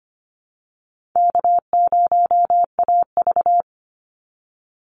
Morse code, heard as a keyed tone, about 25 wpm, 700 Hz. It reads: K0AV
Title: Call Signs Repeated 2x - 25wpm